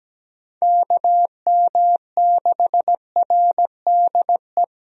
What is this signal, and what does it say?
Morse code, heard as a keyed tone, about 17 wpm, 700 Hz. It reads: KM6RDE
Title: USA Call Signs - 6 Characters - Repeated 2x - 17wpm